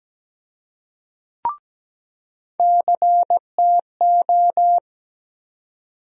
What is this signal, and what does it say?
Morse code, heard as a keyed tone, about 17 wpm, 700 Hz. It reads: CTO
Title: USA Call Sign Postfixes - 3 Letters 17wpm